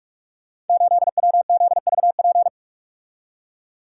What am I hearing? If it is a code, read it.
Morse code, heard as a keyed tone, about 44 wpm, 700 Hz. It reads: 8WZVP
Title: Code Groups - 5 Letter:Numbers - 45wpm